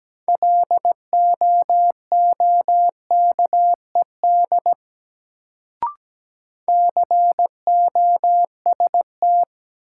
Morse code, heard as a keyed tone, about 17 wpm, 700 Hz. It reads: LOOKED COST
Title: Sentences from Top 800 Words - New Word Drill - 17wpm